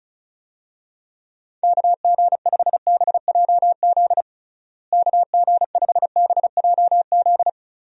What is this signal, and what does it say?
Morse code, heard as a keyed tone, about 36 wpm, 700 Hz. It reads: KG5BJZ KG5BJZ
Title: USA Call Signs - 6 Characters - Repeated 2x - 35wpm